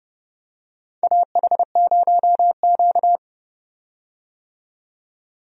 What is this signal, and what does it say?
Morse code, heard as a keyed tone, about 30 wpm, 700 Hz. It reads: AH0Q
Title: USA Call Signs - 4 Characters - 30wpm